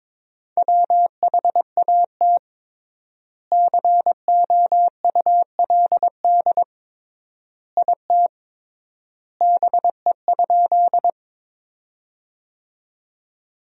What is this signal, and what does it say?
Morse code, heard as a keyed tone, about 22 wpm, 700 Hz. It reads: WHAT COULD IT BE?
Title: 3x Word-Spacing - Sentences from Top 100 Words - 22wpm